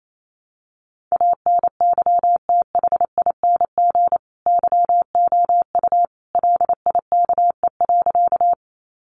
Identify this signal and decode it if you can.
Morse code, heard as a keyed tone, about 28 wpm, 700 Hz. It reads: ANYTHING YOU LIKE.
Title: Sentences from Top 1000 Words 28wpm